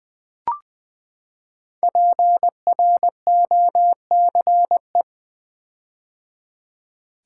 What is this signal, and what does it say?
Morse code, heard as a keyed tone, about 20 wpm, 700 Hz. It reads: PROCE
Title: Build A Word - 5 Letters At A Time - 10-Letter Words - 20wpm